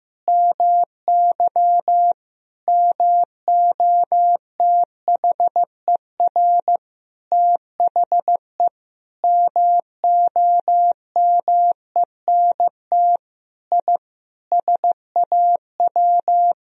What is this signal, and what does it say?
Morse code, heard as a keyed tone, about 15 wpm, 700 Hz. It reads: MY MOTHER THE MOMENT I SAW
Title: Sentences from Top 1000 Words - Review Part 1 - 15wpm